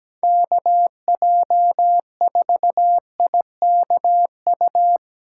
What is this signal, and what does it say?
Morse code, heard as a keyed tone, about 17 wpm, 700 Hz. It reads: KJ4IKU
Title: USA Callsign DE Callsign - 6 Characters - 17wpm